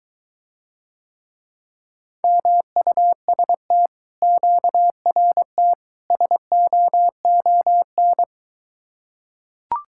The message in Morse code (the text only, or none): MUST QRT SOON